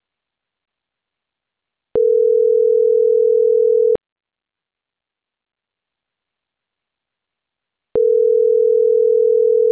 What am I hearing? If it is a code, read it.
Ringback tone (North America), 440 + 480 Hz.